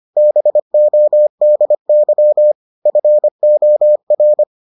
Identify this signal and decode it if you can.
Morse code, heard as a keyed tone, about 25 wpm, 600 Hz. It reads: BODY FOR